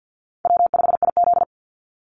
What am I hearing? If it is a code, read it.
Morse code, heard as a keyed tone, about 50 wpm, 700 Hz. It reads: W5IZ
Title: Call Signs - 50wpm